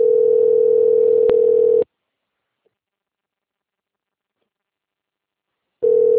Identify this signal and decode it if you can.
Ringback tone (North America), 440 + 480 Hz.
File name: out- - - - - .355